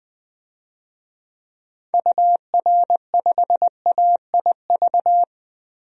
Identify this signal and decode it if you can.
Morse code, heard as a keyed tone, about 21 wpm, 700 Hz. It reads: UR5AIV
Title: DXCC-100 Call Signs - 6 Characters - 20wpm